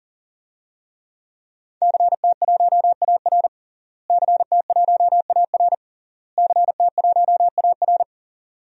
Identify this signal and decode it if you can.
Morse code, heard as a keyed tone, about 40 wpm, 700 Hz. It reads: CT1AR CT1AR CT1AR
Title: DXCC-100 Call Signs - 5 Characters - Repeated 3x - 40wpm